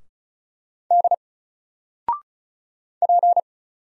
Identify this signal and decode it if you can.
Morse code, heard as a keyed tone, about 35 wpm, 700 Hz. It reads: D P
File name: Single-Letter-Number-35wpm